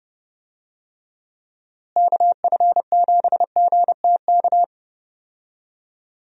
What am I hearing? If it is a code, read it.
Morse code, heard as a keyed tone, about 30 wpm, 700 Hz. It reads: KF7GTK
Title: USA Call Signs - 6 Characters - Repeated 2x - 30wpm